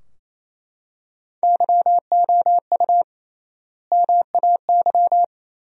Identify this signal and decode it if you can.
Morse code, heard as a keyed tone, about 28 wpm, 700 Hz. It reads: YOU MAY
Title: 3x Word-Spacing - Sentences from Top 100 Words - 28wpm